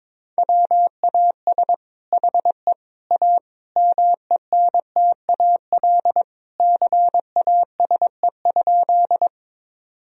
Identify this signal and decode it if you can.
Morse code, heard as a keyed tone, about 23 wpm, 700 Hz. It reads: WAS HE A MENTAL CASE?
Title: Sentences from Top 1000 Words - Review Part 4 - 22wpm